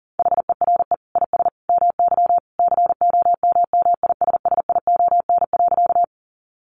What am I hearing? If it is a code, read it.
Morse code, heard as a keyed tone, about 40 wpm, 700 Hz. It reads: HERE IS MY COMMISSION.
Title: Sentences from Top 1400 Words 40wpm